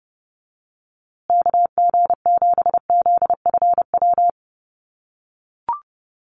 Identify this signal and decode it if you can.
Morse code, heard as a keyed tone, about 29 wpm, 700 Hz. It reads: KG7ZFW
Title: USA Call Signs Repeated 2x 30wpm